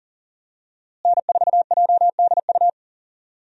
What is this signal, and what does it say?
Morse code, heard as a keyed tone, about 40 wpm, 700 Hz. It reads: N4JDU
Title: USA Call Signs - 5 Characters - Repeated 2x - 40wpm